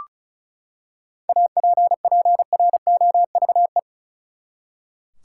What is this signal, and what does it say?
Morse code, heard as a keyed tone, about 36 wpm, 700 Hz. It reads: APPROVE
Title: Seven Letter Words - 35wpm